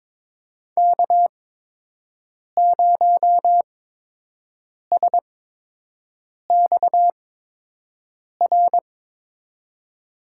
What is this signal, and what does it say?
Morse code, heard as a keyed tone, about 22 wpm, spaced as for 6 wpm, 700 Hz. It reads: K0SXR
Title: USA Call Signs - 8x Character Spacing - 22wpm